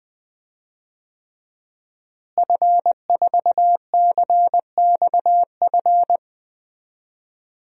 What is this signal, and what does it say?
Morse code, heard as a keyed tone, about 20 wpm, 700 Hz. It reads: F4CXF
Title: DXCC-100 Call Signs - 5 Characters - Repeated 3x - 20wpm